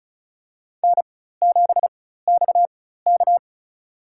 Morse code, heard as a keyed tone, about 34 wpm, spaced as for 16 wpm, 700 Hz. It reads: N7XK
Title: USA Call Signs - 4x Character Spacing - 35wpm